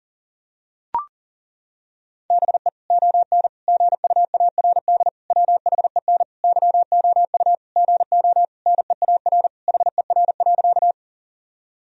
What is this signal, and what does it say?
Morse code, heard as a keyed tone, about 40 wpm, 700 Hz. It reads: BE ON GUARD WHEN YOU GO NEAR HER.
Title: Sentences from Top 1000 Words - Review Part 4 - 40wpm